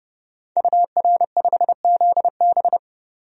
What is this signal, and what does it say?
Morse code, heard as a keyed tone, about 29 wpm, 700 Hz. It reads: UR5ZB
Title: DXCC-100 Call Signs - 5 Characters - 30wpm